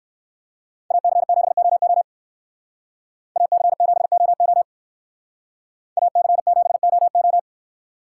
Morse code, heard as a keyed tone, about 59 wpm, 690 Hz. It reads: AC7OO AC7OO AC7OO